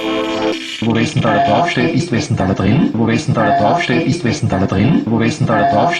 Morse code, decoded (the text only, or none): none